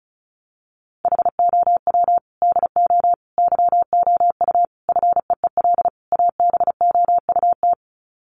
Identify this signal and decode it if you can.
Morse code, heard as a keyed tone, about 34 wpm, 700 Hz. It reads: HOW DO YOU FEEL ABOUT
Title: Sets of 5 Words 35wpm